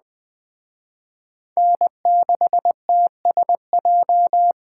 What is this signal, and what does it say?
Morse code, heard as a keyed tone, about 20 wpm, 700 Hz. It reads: N6TSJ
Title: Speed Racing - USA 1x3 Call Signs 17wpm